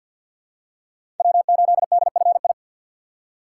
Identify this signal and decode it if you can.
Morse code, heard as a keyed tone, about 47 wpm, 700 Hz. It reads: W8DFI
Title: USA Call Signs - 5 Characters - 6x Character Spacing - 50wpm